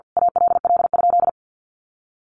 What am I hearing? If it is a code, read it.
Morse code, heard as a keyed tone, about 50 wpm, 700 Hz. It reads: ALL?
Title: Sentences from Top 1500 Words 50wpm